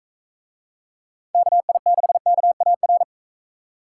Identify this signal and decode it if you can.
Morse code, heard as a keyed tone, about 41 wpm, 700 Hz. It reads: KI6KAR